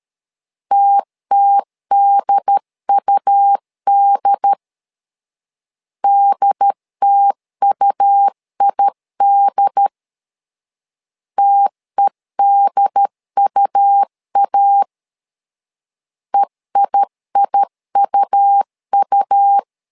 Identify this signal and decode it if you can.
Morse code, heard as a keyed tone, about 13 wpm, 800 Hz. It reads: TTDUD DTUID TEDUA EIIUU
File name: LEZIONI / 2 UHDN / 2 UHD 12wpm 1540